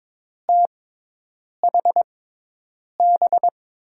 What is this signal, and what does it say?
Morse code, heard as a keyed tone, about 22 wpm, spaced as for 8 wpm, 700 Hz. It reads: THB
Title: USA Call Signs - 6x Character Spacing - 22wpm